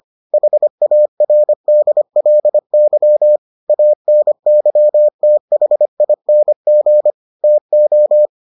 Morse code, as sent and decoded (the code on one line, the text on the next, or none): .... .- .-. -.. .-.. -.-- / .- -. -.-- - .... .. -. --. / - ---
HARDLY ANYTHING TO